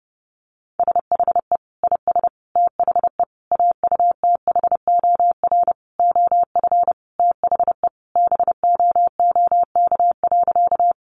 Morse code, heard as a keyed tone, about 30 wpm, 700 Hz. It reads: SHE IS THE AUTHOR OF THE BOOK.
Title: Sentences from Top 1200 Words 30wpm